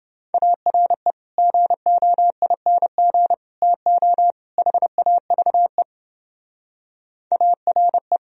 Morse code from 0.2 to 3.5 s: ARE GOING

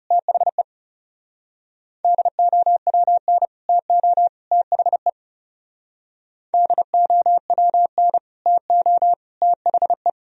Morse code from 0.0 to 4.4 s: THE DOWN TO

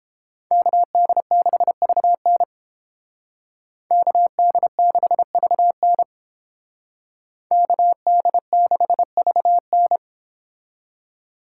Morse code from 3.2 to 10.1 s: KD6VN KD6VN

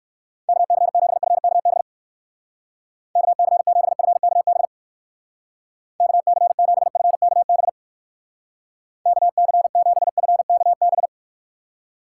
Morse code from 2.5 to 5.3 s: KC7FKB

